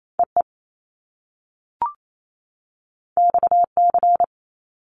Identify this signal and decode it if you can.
Morse code, heard as a keyed tone, about 28 wpm, 700 Hz. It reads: EE XC
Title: Top Two Letter Combinations in Words 28wpm